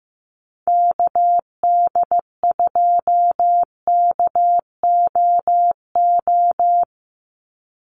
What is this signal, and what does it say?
Morse code, heard as a keyed tone, about 15 wpm, 700 Hz. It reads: KD2KOO
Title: USA Call Signs 15wpm